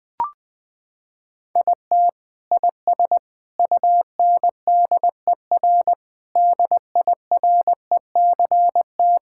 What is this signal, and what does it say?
Morse code, heard as a keyed tone, about 20 wpm, 700 Hz. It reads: IT IS UNDER DIRECT
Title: Sentences from Top 1000 Words - Review Part 5 - 20wpm